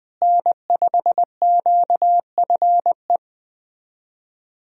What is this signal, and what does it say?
Morse code, heard as a keyed tone, about 20 wpm, 700 Hz. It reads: N5QFE
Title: USA Call Signs - 5 Characters - Repeated 3x - 20wpm